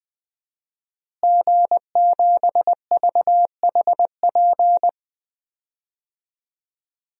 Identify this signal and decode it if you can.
Morse code, heard as a keyed tone, about 20 wpm, 700 Hz. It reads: G7VHP
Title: DXCC-100 Call Signs - 5 Characters - Repeated 2x - 20wpm